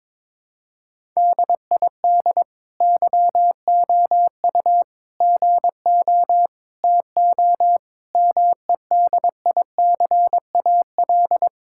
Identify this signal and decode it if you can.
Morse code, heard as a keyed tone, about 22 wpm, 700 Hz. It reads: DID YOU GO TO MEDICAL
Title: Sentences from Top 1000 Words - Review Part 5 - 22wpm